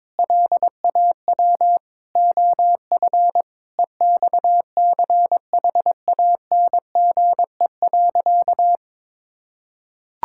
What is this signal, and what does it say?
Morse code, heard as a keyed tone, about 22 wpm, 700 Hz. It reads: LAW OF EXCHANGE.
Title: Sentences from Top 1000 Words - Review Part 6 - 22wpm